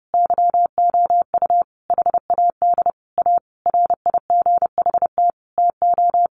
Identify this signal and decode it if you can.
Morse code, heard as a keyed tone, about 30 wpm, 700 Hz. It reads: YOU HAD A RIGHT TO
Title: Sentences from Top 1300 Words 30wpm